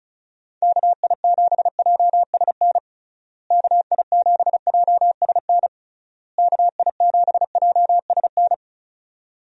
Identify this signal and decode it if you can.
Morse code, heard as a keyed tone, about 36 wpm, 700 Hz. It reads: KI7JSN KI7JSN KI7JSN
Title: USA Call Signs - 6 Characters - Repeated 3x - 35wpm